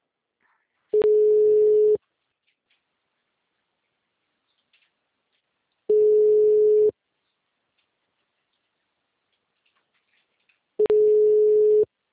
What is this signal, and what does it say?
Ringback tone (Europe, 425 Hz).
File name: out- -2001